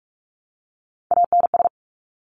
Morse code, heard as a keyed tone, about 46 wpm, 700 Hz. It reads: ANS
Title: Top Three Letter Combinations in Words 45wpm